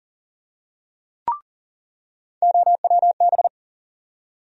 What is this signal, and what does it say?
Morse code, heard as a keyed tone, about 40 wpm, 700 Hz. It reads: OWB
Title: USA Call Sign Postfixes - 3 Letters 40wpm